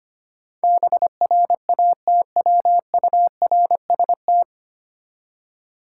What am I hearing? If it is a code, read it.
Morse code, heard as a keyed tone, about 25 wpm, 700 Hz. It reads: BRATWURST
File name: practice / 25wpm / Difficult Words with Hints - 25wpm